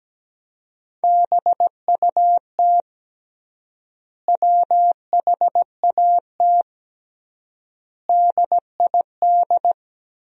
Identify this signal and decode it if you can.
Morse code, heard as a keyed tone, about 17 wpm, 700 Hz. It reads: BUT WHAT DID
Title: 3x Word-Spacing - Sentences from Top 300 Words - 17wpm